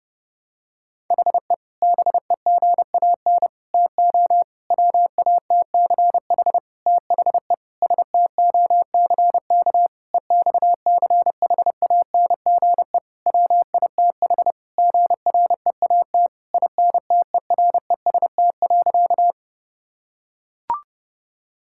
Morse code, heard as a keyed tone, about 32 wpm, 700 Hz. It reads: HE BEGAN TO WATCH THE STOCK EXCHANGE WITH GREAT INTEREST.
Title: Sentences from Top 1000 Words - Review Part 6 - 30wpm